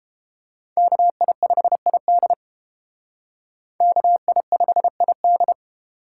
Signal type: Morse code

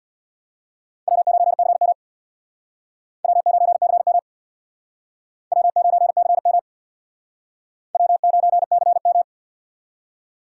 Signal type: Morse code

